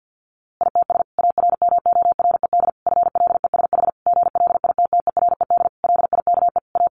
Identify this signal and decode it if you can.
Morse code, heard as a keyed tone, about 50 wpm, 700 Hz. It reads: ITS ARMORED FLESH GLITTERED LIKE A
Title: A Sound Of Thunder 50wpm